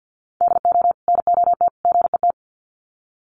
Morse code, heard as a keyed tone, about 50 wpm, 700 Hz. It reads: DO NOT GET
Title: Sets of 3 Words 50wpm